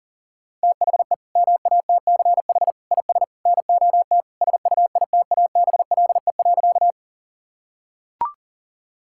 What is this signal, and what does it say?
Morse code, heard as a keyed tone, about 40 wpm, 700 Hz. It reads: THE MATCH IS NOT SUITABLE.
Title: Sentences from Top 1600 Words 40wpm